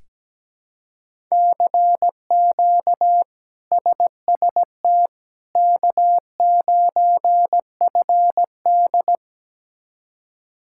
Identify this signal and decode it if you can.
Morse code, heard as a keyed tone, about 17 wpm, 700 Hz. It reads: CQ SST K9FD
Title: K1USN SST QSO Practice - 17wpm